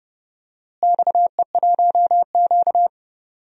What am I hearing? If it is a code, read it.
Morse code, heard as a keyed tone, about 30 wpm, 700 Hz. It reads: XE1Q
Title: DXCC-100 Call Signs - 4 Characters - Repeated 2x - 30wpm